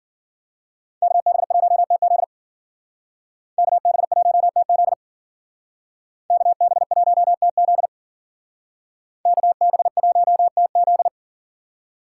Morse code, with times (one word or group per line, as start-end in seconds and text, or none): 1.0-2.2 KB1TZ
3.6-4.9 KB1TZ
6.3-7.9 KB1TZ
9.3-11.1 KB1TZ